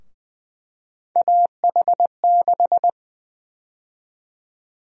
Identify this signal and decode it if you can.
Morse code, heard as a keyed tone, about 20 wpm, 700 Hz. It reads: AH6
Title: USA Call Sign Prefixes - Repeated 2x - 20wpm